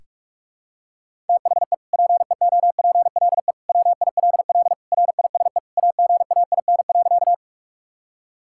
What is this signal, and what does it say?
Morse code, heard as a keyed tone, about 48 wpm, 700 Hz. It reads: THE PEOPLE WILL RISE AGAIN.
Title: Sentences from Top 1000 Words - Review Part 6 - 45wpm